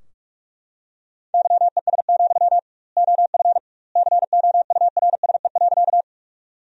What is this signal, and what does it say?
Morse code, heard as a keyed tone, about 41 wpm, 700 Hz. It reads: YES, OF COURSE.
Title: 2x Word-Spacing - Sentences from Top 500 Words - 45wpm